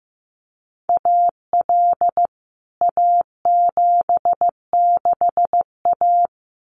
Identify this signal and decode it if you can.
Morse code, heard as a keyed tone, about 15 wpm, 700 Hz. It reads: AL A76A
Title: CWT Exchange 15wpm